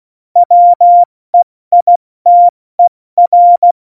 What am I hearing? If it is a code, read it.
Morse code, heard as a keyed tone, about 16 wpm, spaced as for 14 wpm, 700 Hz. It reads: WEITER
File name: Morsekurs / Witze / Witz 29. September 2020, 70 F80